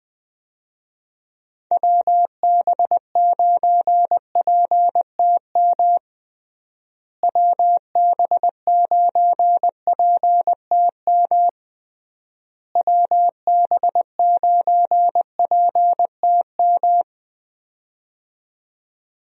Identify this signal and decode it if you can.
Morse code, heard as a keyed tone, about 20 wpm, 700 Hz. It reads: WB9PTM WB9PTM WB9PTM
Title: USA Call Signs - 6 Characters - Repeated 3x - 20wpm